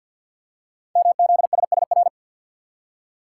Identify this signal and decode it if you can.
Morse code, heard as a keyed tone, about 50 wpm, 700 Hz. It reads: MZSSR
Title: Code Groups - 5 Letters - 50wpm